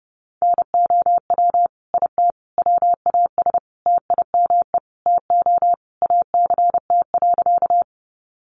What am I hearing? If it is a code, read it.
Morse code, heard as a keyed tone, about 30 wpm, 700 Hz. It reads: NOW IT WAS TIME TO ACT.
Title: Sentences from Top 200 Words 30wpm